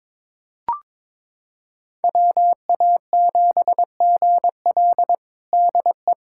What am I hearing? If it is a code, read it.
Morse code, heard as a keyed tone, about 22 wpm, 700 Hz. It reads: WA7GL DE
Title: USA Callsign DE Callsign - 5 Characters - 22wpm